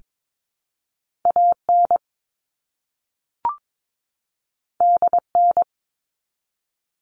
Morse code, heard as a keyed tone, about 22 wpm, 700 Hz. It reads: AN DN
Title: All Cut Numbers - 2-Digit 22wpm